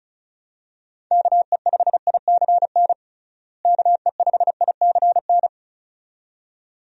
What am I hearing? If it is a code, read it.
Morse code, heard as a keyed tone, about 36 wpm, 700 Hz. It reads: KE5ICN KE5ICN
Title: USA Call Signs - 6 Characters - Repeated 2x - 35wpm